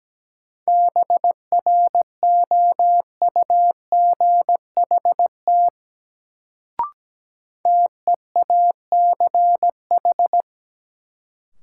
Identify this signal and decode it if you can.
Morse code, heard as a keyed tone, about 17 wpm, 700 Hz. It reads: BROUGHT TEACH
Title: Sentences from Top 500 Words - New Word Drill - 17wpm